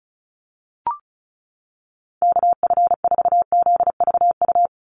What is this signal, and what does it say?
Morse code, heard as a keyed tone, about 36 wpm, 700 Hz. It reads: KF4ZVU
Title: USA Call Signs 35wpm